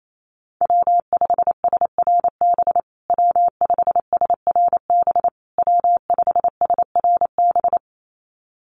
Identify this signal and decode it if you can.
Morse code, heard as a keyed tone, about 28 wpm, 700 Hz. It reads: W5SRB W5SRB W5SRB
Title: USA Call Signs Repeated 3x 28wpm